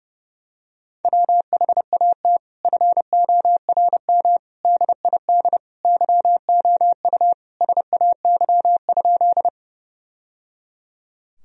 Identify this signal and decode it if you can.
Morse code, heard as a keyed tone, about 30 wpm, 700 Hz. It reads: WHAT FORM DID YOU SAY?
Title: Sentences from Top 1000 Words - Review Part 1 - 30wpm